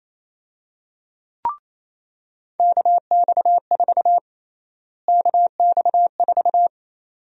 Morse code, heard as a keyed tone, about 28 wpm, 700 Hz. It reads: KX4 KX4
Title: USA Call Sign Prefixes - Repeated 2x - 28wpm